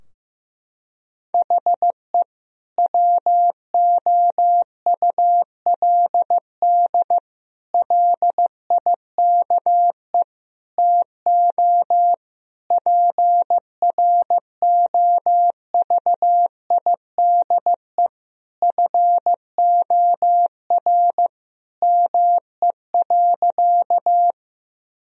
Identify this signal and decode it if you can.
Morse code, heard as a keyed tone, about 15 wpm, 700 Hz. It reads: HE WOULD LIKE TO PROVIDE FOR ME.
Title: Sentences from Top 1000 Words - Review Part 3 - 15wpm